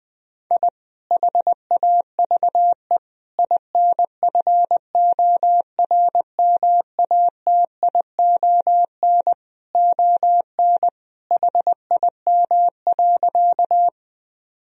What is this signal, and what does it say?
Morse code, heard as a keyed tone, about 20 wpm, 700 Hz. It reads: I HAVE INFORMATION ON HIM.
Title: Sentences from Top 1000 Words - Review Part 3 - 20wpm